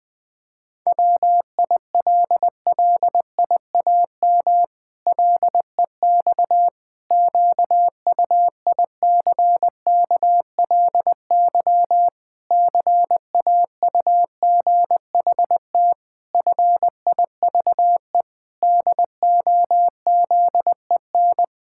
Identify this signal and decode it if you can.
Morse code, heard as a keyed tone, about 20 wpm, 700 Hz. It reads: WILLIAM LEX QUICKLY CAUGHT FIVE DOZEN
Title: Pangram Challenge v2 - 20wpm